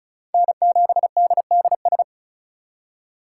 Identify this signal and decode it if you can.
Morse code, heard as a keyed tone, about 46 wpm, 700 Hz. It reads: N7DDS